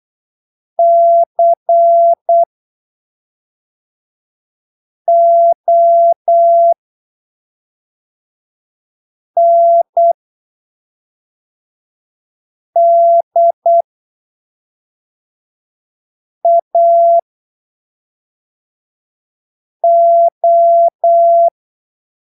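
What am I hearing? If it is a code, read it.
Morse code, heard as a keyed tone, about 8 wpm, spaced as for 3 wpm, 680 Hz. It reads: CONDAO
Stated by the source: Morse training text at 8 ppm for Brazilian class B amateur radio telegraphy exams, from Portal do CW, PY2BRL.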